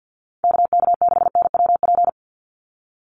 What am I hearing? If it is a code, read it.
Morse code, heard as a keyed tone, about 50 wpm, 700 Hz. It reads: KK6NWP
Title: USA Call Signs 50wpm